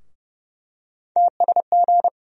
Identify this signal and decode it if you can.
Morse code, heard as a keyed tone, about 30 wpm, 700 Hz. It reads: TSG
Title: USA Call Sign Postfixes - 3 Letters 30wpm